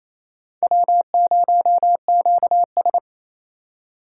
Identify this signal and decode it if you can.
Morse code, heard as a keyed tone, about 37 wpm, 700 Hz. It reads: W0QS